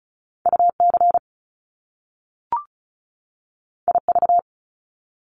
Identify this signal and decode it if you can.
Morse code, heard as a keyed tone, about 36 wpm, 700 Hz. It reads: UC IV
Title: Top Two Letter Combinations in Words 35wpm